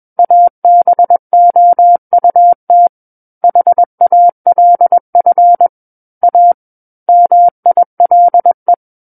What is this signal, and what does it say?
Morse code, heard as a keyed tone, about 21 wpm, 700 Hz. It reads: ABOUT HALF A MILE
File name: Book One / 12-Chapter 12 - 21 WPM